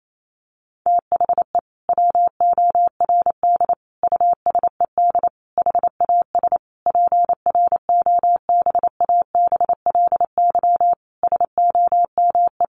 Morse code, heard as a keyed tone, about 28 wpm, 700 Hz. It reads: THE WORD USED HAS PROBABLY SOME
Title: Sentences from Top 1100 Words 28wpm